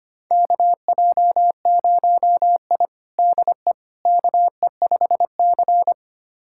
Morse code, heard as a keyed tone, about 25 wpm, 700 Hz. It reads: KJ0I DE KE5C
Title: USA Callsign DE Callsign - 4 Characters - 25wpm